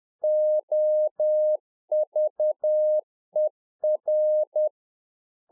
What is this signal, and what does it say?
Morse code, heard as a keyed tone, about 10 wpm, 610 Hz. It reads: OVER
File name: Czech Army Morse Keys 10WPM